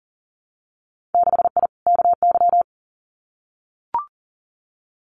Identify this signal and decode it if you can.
Morse code, heard as a keyed tone, about 40 wpm, 700 Hz. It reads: 6I KY
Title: Winter Field Day Exchanges 40wpm